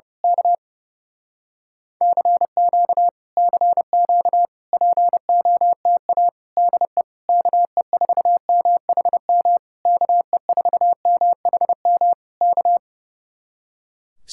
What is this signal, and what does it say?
Morse code, heard as a keyed tone, about 30 wpm, 700 Hz. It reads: K CQ CQ POTA DE KE4MHM KE4MHM K
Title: Speed-Racing Common QSO Phrases Encore - 30wpm